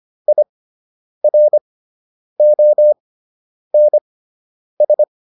Morse code, heard as a keyed tone, about 25 wpm, spaced as for 9 wpm, 600 Hz. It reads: IRONS